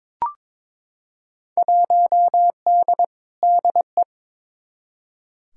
Morse code, heard as a keyed tone, about 22 wpm, 700 Hz. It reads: 1D DE
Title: Field Day Exchanges - 22wpm